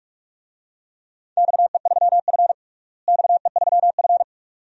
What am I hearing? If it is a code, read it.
Morse code, heard as a keyed tone, about 46 wpm, 700 Hz. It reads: XE3F XE3F
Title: DXCC-100 Call Signs - 4 Characters - Repeated 2x - 45wpm